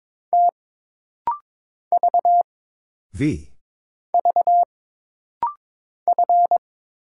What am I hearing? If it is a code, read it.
Morse code, heard as a keyed tone, about 22 wpm, 700 Hz. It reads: T V V F
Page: This rapid-fire timing is geared to push your ICR (Instant Character Recognition) skills.